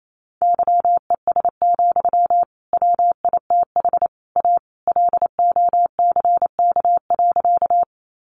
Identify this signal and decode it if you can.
Morse code, heard as a keyed tone, about 28 wpm, 700 Hz. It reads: YES, WITH A LOCK.
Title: Sentences from Top 1300 Words 28wpm